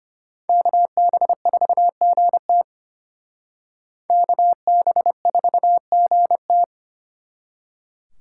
Morse code, heard as a keyed tone, about 32 wpm, 700 Hz. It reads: KB4GT KB4GT